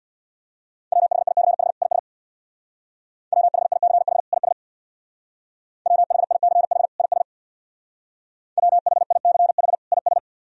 Morse code, heard as a keyed tone, about 63 wpm, 700 Hz. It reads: WHICH IS WHICH IS WHICH IS WHICH IS